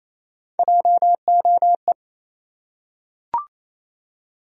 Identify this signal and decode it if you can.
Morse code, heard as a keyed tone, about 28 wpm, 700 Hz. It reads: JOE
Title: USA Common Names - Max 5 Letters - 28wpm